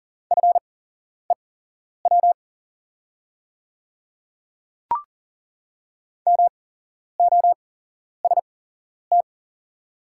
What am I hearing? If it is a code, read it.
Morse code, heard as a keyed tone, about 40 wpm, spaced as for 11 wpm, 700 Hz. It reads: FEW MOST
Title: 8x Character Spacing - Top 300 Words - 40wpm